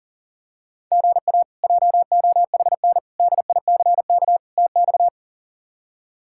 Morse code, heard as a keyed tone, about 40 wpm, 700 Hz. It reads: GA JOHN DICK TX
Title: K1USN SST QSO Practice - 40wpm